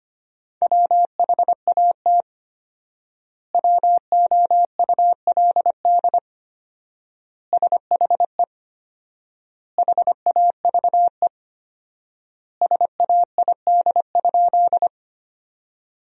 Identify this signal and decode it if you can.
Morse code, heard as a keyed tone, about 25 wpm, 700 Hz. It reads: WHAT WOULD SHE HAVE SAID?
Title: 4x Word-Spacing - Sentences from Top 100 Words - 25wpm